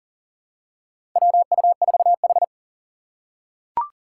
Morse code, heard as a keyed tone, about 40 wpm, 700 Hz. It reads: WU4H
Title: USA Call Signs - 2x Character Spacing - 40wpm